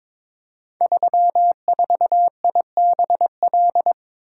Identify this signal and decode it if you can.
Morse code, heard as a keyed tone, about 21 wpm, 700 Hz. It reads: 34IBL
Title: Code Groups - 5 Letter:Numbers - 22wpm